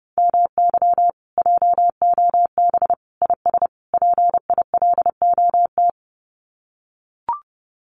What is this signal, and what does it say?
Morse code, heard as a keyed tone, about 28 wpm, 700 Hz. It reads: MY JOB IS PILOT
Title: Common QSO Phrases Encore 30wpm